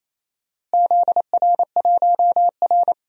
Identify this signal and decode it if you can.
Morse code, heard as a keyed tone, about 28 wpm, 700 Hz. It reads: ZR1R
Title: DXCC-100 Call Signs - 4 Characters - Repeated 3x - 28wpm